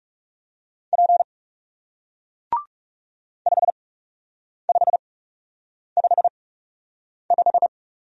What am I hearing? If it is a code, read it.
Morse code, heard as a keyed tone, about 41 wpm, spaced as for 8 wpm, 700 Hz. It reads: P 5555